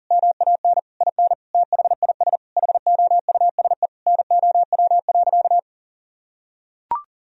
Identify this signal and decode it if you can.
Morse code, heard as a keyed tone, about 40 wpm, 700 Hz. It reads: MAN IN THIS HOUSE NOW.
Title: Sentences from Top 1000 Words - Review Part 2 - 40wpm